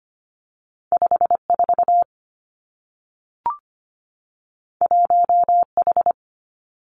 Morse code, heard as a keyed tone, about 25 wpm, 700 Hz. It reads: 54 1H
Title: Two Challenging Letter-Number Combinations 25wpm